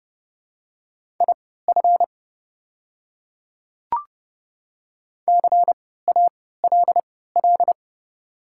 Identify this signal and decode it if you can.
Morse code, heard as a keyed tone, about 30 wpm, spaced as for 17 wpm, 690 Hz. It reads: IF CALL
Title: 3x Character Spacing - Top 100 Words - 30wpm